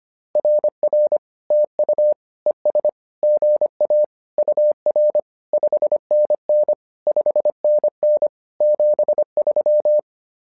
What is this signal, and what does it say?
Morse code, heard as a keyed tone, about 25 wpm, 600 Hz. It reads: RR TU ES GA UR 5NN 5NN 73
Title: SOTA QSO Practice 25wpm